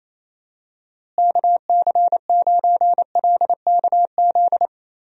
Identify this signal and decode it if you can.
Morse code, heard as a keyed tone, about 27 wpm, 700 Hz. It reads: KC9LKZ
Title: USA Call Signs - 6 Characters - Repeated 2x - 28wpm